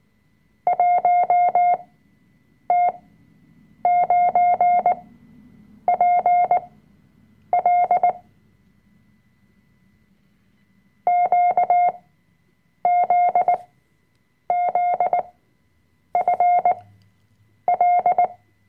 Morse code, heard as a keyed tone, about 18 wpm, spaced as for 7 wpm, 710 Hz. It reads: 1T9PL QZZFL
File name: Workshop Dateien / CW-Online-Kurs / Lektion 005